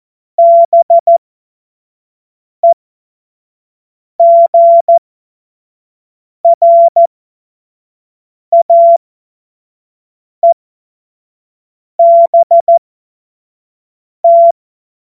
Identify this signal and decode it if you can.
Morse code, heard as a keyed tone, about 14 wpm, spaced as for 5 wpm, 680 Hz. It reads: BEGRAEBT
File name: Morsekurs / Witze / Witz 27. Januar 2026 25 F70